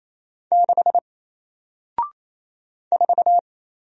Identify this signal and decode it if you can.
Morse code, heard as a keyed tone, about 28 wpm, 700 Hz. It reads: 6 4
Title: Single Character (Letter:Number:Punctuation:Prosign) Alt 28wpm